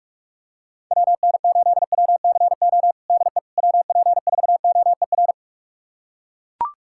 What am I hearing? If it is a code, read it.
Morse code, heard as a keyed tone, about 42 wpm, 700 Hz. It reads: WN8WCO DE WP4OER
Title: USA Callsign DE Callsign - 6 Characters - 45wpm